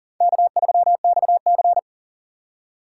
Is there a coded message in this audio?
Yes: Morse code from 0.2 to 1.8 s, heard as a keyed tone, about 51 wpm, 700 Hz. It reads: K3XC